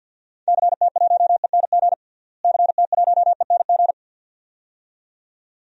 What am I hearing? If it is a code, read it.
Morse code, heard as a keyed tone, about 47 wpm, 700 Hz. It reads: CT1ENG CT1ENG
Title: DXCC-100 Call Signs - 6 Characters - Repeated 2x - 50wpm